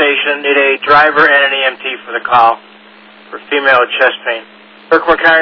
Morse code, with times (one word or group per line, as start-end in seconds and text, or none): none